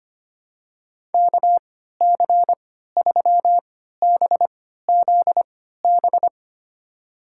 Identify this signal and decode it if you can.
Morse code, heard as a keyed tone, about 25 wpm, spaced as for 14 wpm, 700 Hz. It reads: KC3BZB
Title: USA Call Signs - 6 Characters - 3x Character Spacing - 25wpm